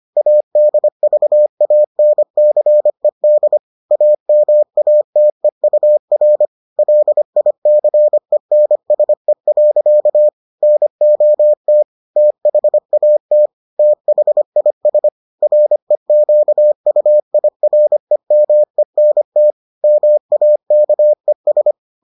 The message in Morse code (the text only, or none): ADVANCED AMATEUR LICENSE. NOT THAT THIS REQUIREMENT MAKES